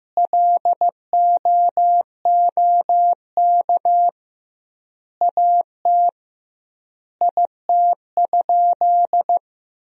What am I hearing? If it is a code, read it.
Morse code, heard as a keyed tone, about 15 wpm, 700 Hz. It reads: LOOK AT IT?
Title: 2x Word-Spacing - Sentences from Top 100 Words - 15wpm